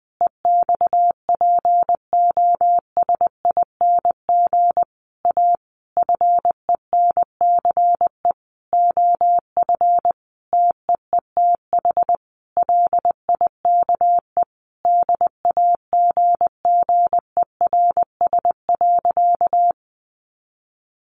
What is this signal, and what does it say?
Morse code, heard as a keyed tone, about 20 wpm, 700 Hz. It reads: EXPOSING A FENCE OF TEETH LIKE DAGGERS.
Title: A Sound Of Thunder 20wpm